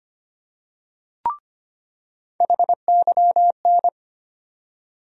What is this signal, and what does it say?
Morse code, heard as a keyed tone, about 25 wpm, 700 Hz. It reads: HYN
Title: USA Call Sign Postfixes - 3 Letters 25wpm